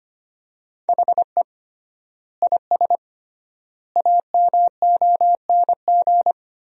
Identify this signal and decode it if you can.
Morse code, heard as a keyed tone, about 25 wpm, 700 Hz. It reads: HE IS AMONG
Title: 3x Word-Spacing - Sentences from Top 500 Words - 25wpm